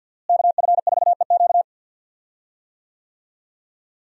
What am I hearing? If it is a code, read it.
Morse code, heard as a keyed tone, about 47 wpm, 700 Hz. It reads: KF4EQ
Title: USA Call Signs - 5 Characters - Repeated 2x - 50wpm